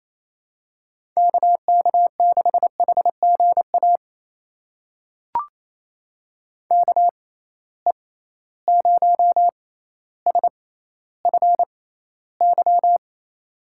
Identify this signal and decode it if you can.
Morse code, heard as a keyed tone, about 28 wpm, spaced as for 14 wpm, 700 Hz. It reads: KK6HGA KE0SFY
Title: USA Call Signs - 6 Characters - 6x Character Spacing - 28wpm